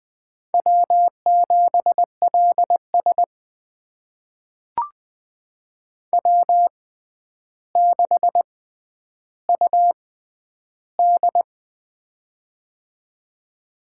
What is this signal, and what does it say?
Morse code, heard as a keyed tone, about 20 wpm, spaced as for 10 wpm, 700 Hz. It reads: W7LS W6UD
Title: USA Call Signs - 4 Characters - 6x Character Spacing - 20wpm